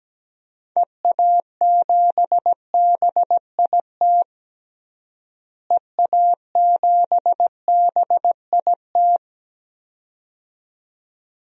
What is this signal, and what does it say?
Morse code, heard as a keyed tone, about 17 wpm, 700 Hz. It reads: EA7BIT EA7BIT
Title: DXCC-100 Call Signs - 6 Characters - Repeated 2x - 17wpm